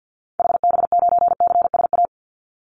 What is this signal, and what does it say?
Morse code, heard as a keyed tone, about 50 wpm, 700 Hz. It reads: HB9CSA
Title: Call Signs - 50wpm